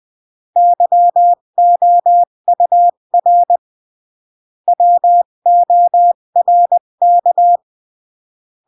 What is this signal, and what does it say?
Morse code, heard as a keyed tone, about 20 wpm, 700 Hz. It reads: YOUR WORK
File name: CW / jodorowsky / 82 commandments 20-15